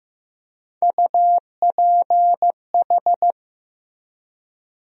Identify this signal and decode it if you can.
Morse code, heard as a keyed tone, about 15 wpm, 700 Hz. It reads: UPH
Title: USA Call Sign Postfixes - 3 Letters 15wpm